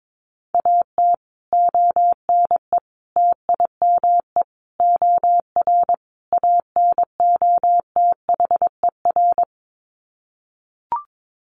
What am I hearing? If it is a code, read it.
Morse code, heard as a keyed tone, about 22 wpm, 700 Hz. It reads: AT ONE TIME OR ANOTHER
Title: Sets of 5 Words Encore 22wpm